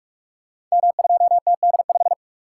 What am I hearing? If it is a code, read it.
Morse code, heard as a keyed tone, about 44 wpm, 700 Hz. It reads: M2TD5
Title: Code Groups - 5 Letter:Numbers - 45wpm